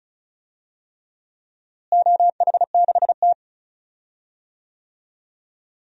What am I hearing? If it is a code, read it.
Morse code, heard as a keyed tone, about 34 wpm, 700 Hz. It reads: OH6T
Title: DXCC-100 Call Signs - 4 Characters - Repeated 3x - 35wpm